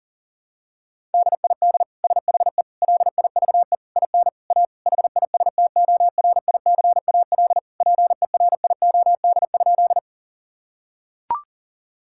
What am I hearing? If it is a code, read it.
Morse code, heard as a keyed tone, about 40 wpm, 700 Hz. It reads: DID SHE LIVE IN A HISTORICAL PERIOD?
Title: Sentences from Top 1000 Words - Review Part 5 - 40wpm